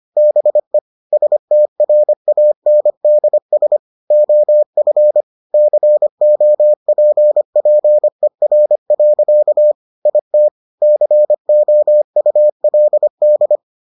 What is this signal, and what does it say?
Morse code, heard as a keyed tone, about 25 wpm, 600 Hz. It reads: BE STRANDS OF COPPER. IT COULD